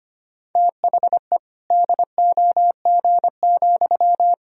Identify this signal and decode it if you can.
Morse code, heard as a keyed tone, about 25 wpm, 700 Hz. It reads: THE DOG,
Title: Sentences from Top 1000 Words - Review Part 3 - 25wpm